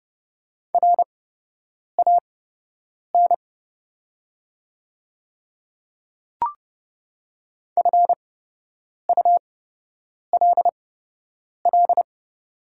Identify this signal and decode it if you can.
Morse code, heard as a keyed tone, about 30 wpm, spaced as for 8 wpm, 690 Hz. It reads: RAN FULL
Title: 8x Character Spacing - Top 500 Words - 30wpm